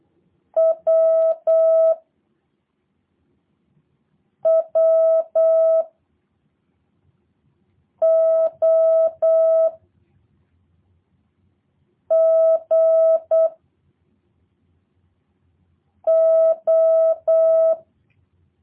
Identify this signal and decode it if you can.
Morse code, heard as a keyed tone, about 8 wpm, 650 Hz. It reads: W W O G O
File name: jh8cbh / 2023cw / review1-4